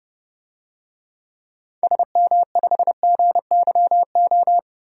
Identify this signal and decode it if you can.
Morse code, heard as a keyed tone, about 30 wpm, 700 Hz. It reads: SM5GYO
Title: DXCC-100 Call Signs - 6 Characters - 30wpm